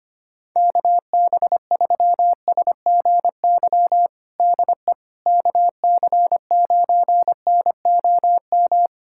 Morse code, heard as a keyed tone, about 25 wpm, 700 Hz. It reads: KB3SGY DE KC9NOM
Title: USA Callsign DE Callsign - 6 Characters - 25wpm